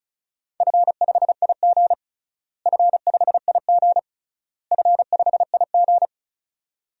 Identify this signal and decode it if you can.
Morse code, heard as a keyed tone, about 36 wpm, 700 Hz. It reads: F5IG F5IG F5IG
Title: DXCC-100 Call Signs - 4 Characters - Repeated 3x - 35wpm